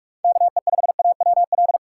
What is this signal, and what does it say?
Morse code, heard as a keyed tone, about 46 wpm, 700 Hz. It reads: KE5AWL
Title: USA Call Signs - 6 Characters - 4x Character Spacing - 45wpm